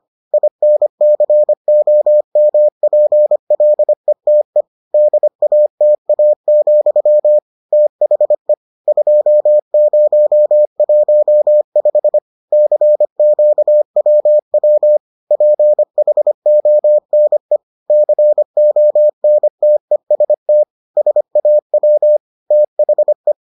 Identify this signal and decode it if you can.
Morse code, heard as a keyed tone, about 25 wpm, 600 Hz. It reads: INCOMPLETE DATA, THE 2015 CQWW PHONE CONTEST SAW THE